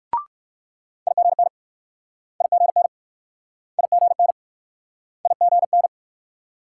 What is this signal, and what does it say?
Morse code, heard as a keyed tone, about 56 wpm, 690 Hz. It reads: IGN IGN IGN IGN